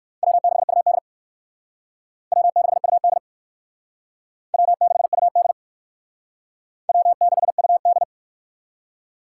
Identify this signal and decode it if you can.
Morse code, heard as a keyed tone, about 58 wpm, 700 Hz. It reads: W6UD W6UD W6UD W6UD